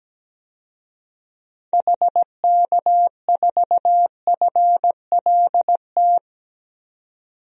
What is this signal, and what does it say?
Morse code, heard as a keyed tone, about 17 wpm, 700 Hz. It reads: HK4FLT
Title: DXCC-100 Call Signs - 6 Characters - 17wpm